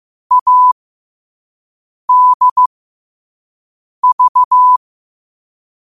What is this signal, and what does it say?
Morse code, heard as a keyed tone, about 15 wpm, spaced as for 5 wpm, 1000 Hz. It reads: ADV
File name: qotdfiles / 5wpm